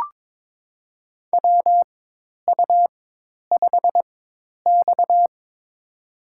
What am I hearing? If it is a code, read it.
Morse code, heard as a keyed tone, about 21 wpm, spaced as for 10 wpm, 700 Hz. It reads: WU5X
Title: USA Call Signs - 4x Character Spacing - 22wpm